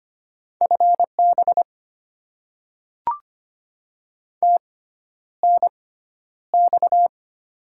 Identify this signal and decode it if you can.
Morse code, heard as a keyed tone, about 25 wpm, spaced as for 11 wpm, 700 Hz. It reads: FB TNX
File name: Top 20 QSO Elements - 6x Character Spacing - 25wpm